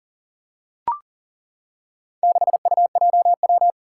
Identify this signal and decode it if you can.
Morse code, heard as a keyed tone, about 42 wpm, 700 Hz. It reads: 6UJW
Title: Four Challenging Letter-Number Combinations 40wpm